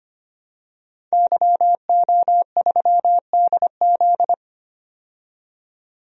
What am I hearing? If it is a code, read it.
Morse code, heard as a keyed tone, about 25 wpm, 700 Hz. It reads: YO3DZ
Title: DXCC-100 Call Signs - 5 Characters - 25wpm